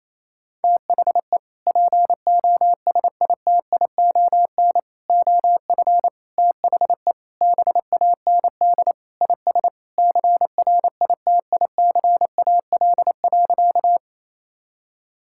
Morse code, heard as a keyed tone, about 28 wpm, 700 Hz. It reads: THE POSITION OF THE BAND IS CRITICAL.
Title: Sentences from Top 1600 Words 28wpm